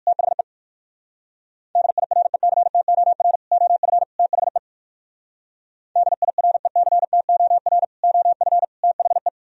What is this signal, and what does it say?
Morse code, heard as a keyed tone, about 57 wpm, 700 Hz. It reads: THE DIRECTOR OF THE DIRECTOR OF THE